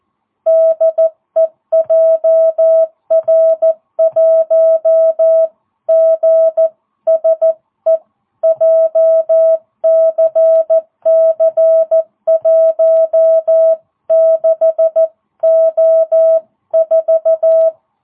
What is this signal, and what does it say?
Morse code, heard as a keyed tone, about 14 wpm, 660 Hz. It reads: DE JR1GSE JCC16O4